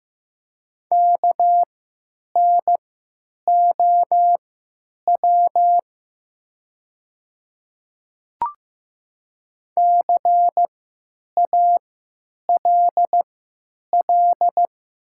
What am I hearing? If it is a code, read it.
Morse code, heard as a keyed tone, about 15 wpm, spaced as for 9 wpm, 700 Hz. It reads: KNOW CALL
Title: 3x Character Spacing - Top 100 Words - 15wpm